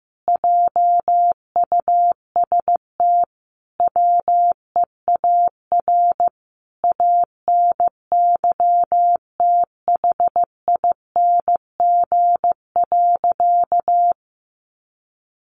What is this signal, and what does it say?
Morse code, heard as a keyed tone, about 15 wpm, 700 Hz. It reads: JUST WEAR ANYTHING.
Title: Sentences from Top 1500 Words 15wpm